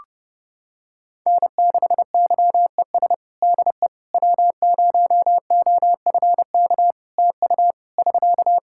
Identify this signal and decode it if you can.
Morse code, heard as a keyed tone, about 30 wpm, 700 Hz. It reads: N6YES DE W0OFK TU <SK>
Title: K1USN SST QSO Practice 30wpm